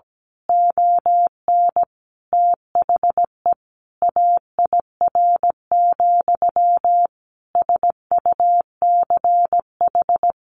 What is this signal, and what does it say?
Morse code, heard as a keyed tone, about 17 wpm, 700 Hz. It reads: ON THE AIR, SUCH
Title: A Sound Of Thunder 17wpm